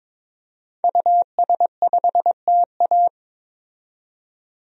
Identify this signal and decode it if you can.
Morse code, heard as a keyed tone, about 21 wpm, 700 Hz. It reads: US5TA
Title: DXCC-100 Call Signs - 5 Characters - 22wpm